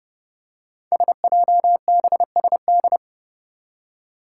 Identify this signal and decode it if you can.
Morse code, heard as a keyed tone, about 30 wpm, 700 Hz. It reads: SJBSD
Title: Five Challenging Letter-Number Combinations 30wpm